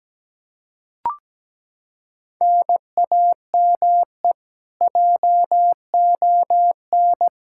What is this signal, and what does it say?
Morse code, heard as a keyed tone, about 17 wpm, 700 Hz. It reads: NAME JON
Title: K1USN SST QSO Practice 17wpm